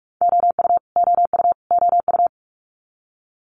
Morse code, heard as a keyed tone, about 46 wpm, 700 Hz. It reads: OU OU OU
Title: Multi-letter Phonemes Repeated 3x 45wpm